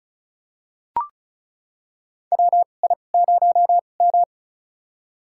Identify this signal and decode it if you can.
Morse code, heard as a keyed tone, about 36 wpm, spaced as for 25 wpm, 700 Hz. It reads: WI0M
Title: USA Call Signs - 4 Characters - 2x Character Spacing - 35wpm